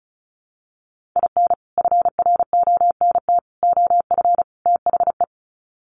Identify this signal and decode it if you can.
Morse code, heard as a keyed tone, about 34 wpm, 700 Hz. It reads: IN FRONT OF THE
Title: Sets of 4 Words 35wpm